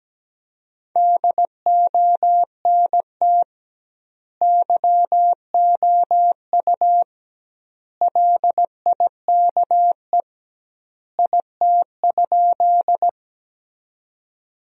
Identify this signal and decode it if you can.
Morse code, heard as a keyed tone, about 17 wpm, 700 Hz. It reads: DONT YOU LIKE IT?
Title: 2x Word-Spacing - Sentences from Top 300 Words - 17wpm